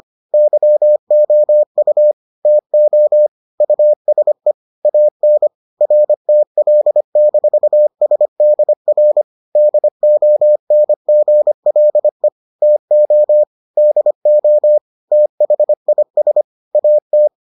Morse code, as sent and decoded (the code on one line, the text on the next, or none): -.-- --- ..- / - --- / ..- ... / .- -. / .-. - .-.. -....- ... -.. .-. / -.. --- -. --. .-.. / - --- / -.. --- / - .... .. ... / .- -
YOU TO USE AN RTL-SDR DONGLE TO DO THIS AT